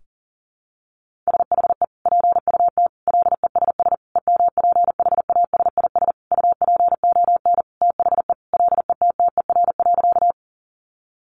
Morse code, heard as a keyed tone, about 40 wpm, 700 Hz. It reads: SHE PUT LESS EMPHASIS UPON THE LETTER.
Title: Sentences from Top 1400 Words 40wpm